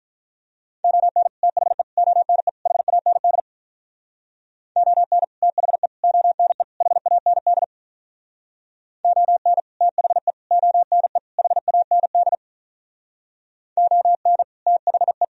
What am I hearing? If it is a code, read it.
Morse code, heard as a keyed tone, about 47 wpm, 700 Hz. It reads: ON THE ONE HAND ON THE ONE HAND ON THE ONE HAND ON THE